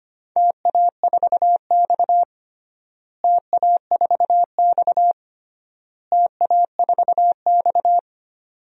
Morse code, heard as a keyed tone, about 25 wpm, 700 Hz. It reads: TA4X TA4X TA4X
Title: DXCC-100 Call Signs - 4 Characters - Repeated 3x - 25wpm